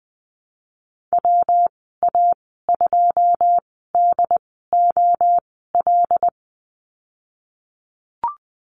Morse code, heard as a keyed tone, about 20 wpm, spaced as for 15 wpm, 700 Hz. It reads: WA2DOL
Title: 167 Lesson 27g - Number 7 - Call Signs 20wpm